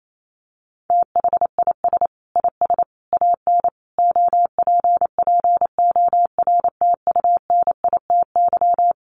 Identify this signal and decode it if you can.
Morse code, heard as a keyed tone, about 28 wpm, 700 Hz. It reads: THIS IS AN OPPORTUNITY
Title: Sentences from Top 1000 Words 28wpm